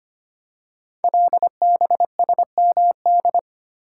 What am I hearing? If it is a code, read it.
Morse code, heard as a keyed tone, about 25 wpm, 700 Hz. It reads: LBSMD
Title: Code Groups - 5 Letters - 25wpm